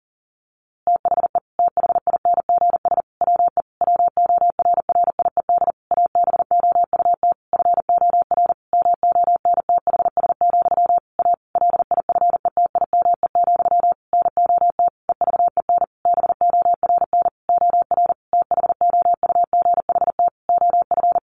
Morse code, heard as a keyed tone, about 40 wpm, 700 Hz. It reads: THE THINGS WE WORRIED ABOUT FOR MONTHS, A LIFETIME, NOT EVEN BORN OR THOUGHT OF
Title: A Sound Of Thunder 40wpm